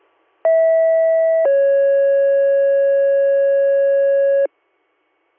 Two-tone page, heard as a one-second tone, then a three-second tone, about 650 then 550 Hz. It is heard from 0.4 to 4.5 s.